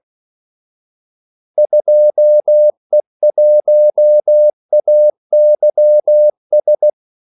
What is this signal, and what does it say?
Morse code, heard as a keyed tone, about 16 wpm, 600 Hz. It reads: 2E1AYS
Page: Kaldesignaler 16 wpm